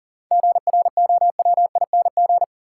Morse code, heard as a keyed tone, about 40 wpm, 700 Hz. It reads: GROWING
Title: Sets of 5 Words Encore - New Word Drill - 40wpm